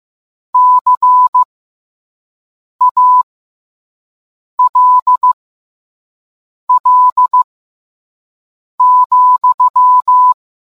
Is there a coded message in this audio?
Yes: Morse code from 0.5 to 10.3 s, heard as a keyed tone, about 15 wpm, spaced as for 5 wpm, 1000 Hz. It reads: CALL,